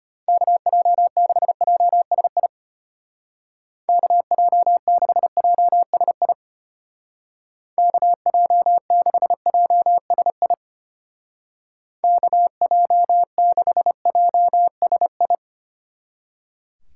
Morse code, heard as a keyed tone, about 32 wpm, 700 Hz. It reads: KJ6JSI KJ6JSI KJ6JSI KJ6JSI